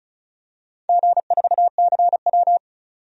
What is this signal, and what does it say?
Morse code, heard as a keyed tone, about 36 wpm, 700 Hz. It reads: G4CW
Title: DXCC-100 Call Signs - 4 Characters - 35wpm